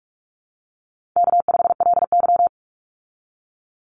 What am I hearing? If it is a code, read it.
Morse code, heard as a keyed tone, about 46 wpm, 700 Hz. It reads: K5LY
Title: Easy Call Signs - 45wpm